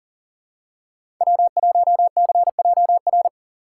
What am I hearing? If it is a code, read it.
Morse code, heard as a keyed tone, about 39 wpm, 700 Hz. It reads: W1CJR
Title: USA Call Signs - 5 Characters - 3x Character Spacing - 40wpm